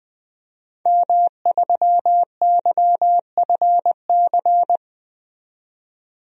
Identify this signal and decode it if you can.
Morse code, heard as a keyed tone, about 20 wpm, 700 Hz. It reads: M3YFC
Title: DXCC-100 Call Signs - 5 Characters - 20wpm